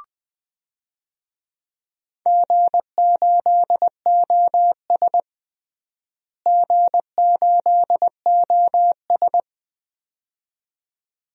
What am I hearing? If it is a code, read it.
Morse code, heard as a keyed tone, about 21 wpm, 700 Hz. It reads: G8OS G8OS
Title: DXCC-100 Call Signs - 4 Characters - Repeated 2x - 20wpm